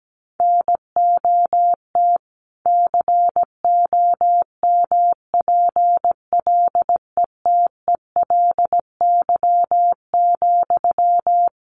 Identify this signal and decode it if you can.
Morse code, heard as a keyed tone, about 17 wpm, 700 Hz. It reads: NOT COMPLETELY,
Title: Sentences from Top 1100 Words 17wpm